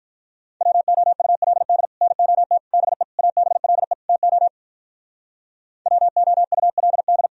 Morse code, heard as a keyed tone, about 46 wpm, 700 Hz. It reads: WOULD NOT BE ABLE TO WOULD